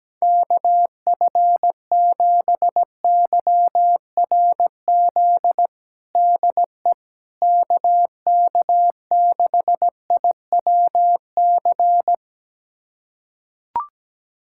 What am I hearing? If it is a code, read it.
Morse code, heard as a keyed tone, about 17 wpm, 700 Hz. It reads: KF7YRZ DE KK6IWC
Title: USA Callsign DE Callsign - 6 Characters - 17wpm